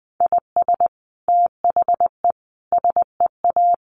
Morse code, heard as a keyed tone, about 20 wpm, 700 Hz. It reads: IS THE SEA
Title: Sentences from Top 1400 Words 20wpm